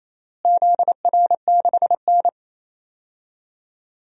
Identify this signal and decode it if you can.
Morse code, heard as a keyed tone, about 28 wpm, 700 Hz. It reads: ZR6N
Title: DXCC-100 Call Signs - 4 Characters - Repeated 3x - 28wpm